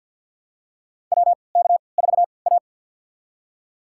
Morse code, heard as a keyed tone, about 51 wpm, spaced as for 29 wpm, 700 Hz. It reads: WK4A
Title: USA Call Signs - 4 Characters - 3x Character Spacing - 50wpm